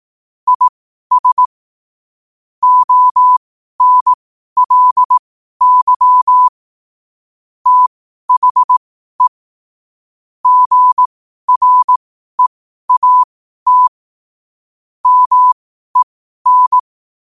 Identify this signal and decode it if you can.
Morse code, heard as a keyed tone, about 18 wpm, spaced as for 12 wpm, 1000 Hz. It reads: IS ONLY THE GREAT MEN
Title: 12 WPM morse code quotes for Fri, 01 Mar 2024